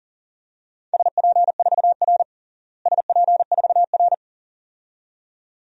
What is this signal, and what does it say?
Morse code, heard as a keyed tone, about 37 wpm, 700 Hz. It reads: SP4R SP4R
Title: DXCC-100 Call Signs - 4 Characters - Repeated 2x - 40wpm